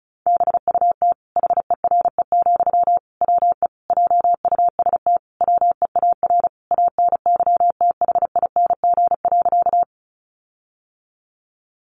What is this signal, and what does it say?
Morse code, heard as a keyed tone, about 34 wpm, 700 Hz. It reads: BUT HERE, WE JUST WEAR ANYTHING.
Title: Sentences from Top 1500 Words 35wpm